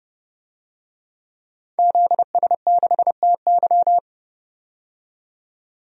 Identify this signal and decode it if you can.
Morse code, heard as a keyed tone, about 29 wpm, 690 Hz. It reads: ZS6TY
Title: DXCC-100 Call Signs - 5 Characters - 30wpm